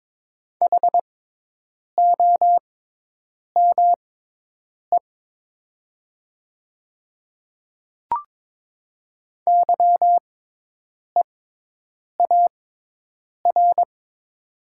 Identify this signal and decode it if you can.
Morse code, heard as a keyed tone, about 22 wpm, spaced as for 8 wpm, 700 Hz. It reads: HOME YEAR
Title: 6x Character Spacing - Top 200 Words - 22wpm